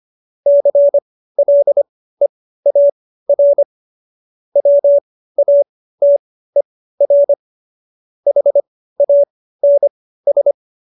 Morse code, heard as a keyed tone, about 25 wpm, spaced as for 15 wpm, 570 Hz. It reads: CLEAR WATER HANS
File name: qso222 15